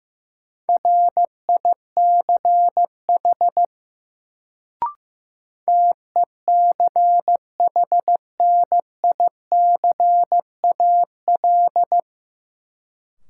Morse code, heard as a keyed tone, about 15 wpm, 700 Hz. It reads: RICH TECHNICAL
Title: Sentences from Top 1100 Words - New Word Drill Alt - 15wpm